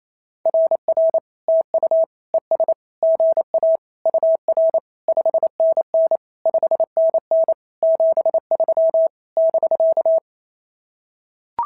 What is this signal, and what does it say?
Morse code, heard as a keyed tone, about 29 wpm, 690 Hz. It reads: RR TU ES GA UR 5NN 5NN 73 <BK>
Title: SOTA QSO Practice 28wpm